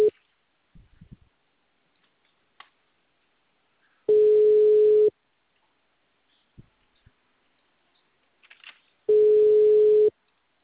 Ringback tone (Europe, 425 Hz).
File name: out- -2007- - - .163